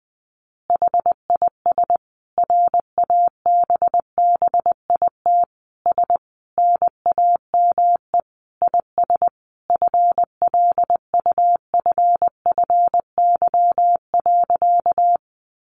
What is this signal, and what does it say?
Morse code, heard as a keyed tone, about 20 wpm, 700 Hz. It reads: HIS RABBIT S NAME IS FLUFFY.
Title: Easy Phrases 20wpm